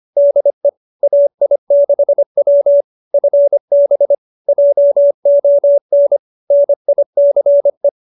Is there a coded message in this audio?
Yes: Morse code from 0.2 to 7.9 s, heard as a keyed tone, about 25 wpm, 570 Hz. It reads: DE AI6W FB JON NICE